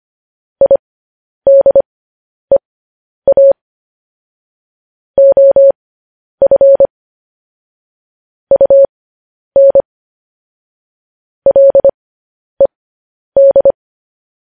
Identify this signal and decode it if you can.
Morse code, heard as a keyed tone, about 25 wpm, spaced as for 10 wpm, 550 Hz. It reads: IDEA OF UN LED
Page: news-headlines-in-morse-code-at-10-wpm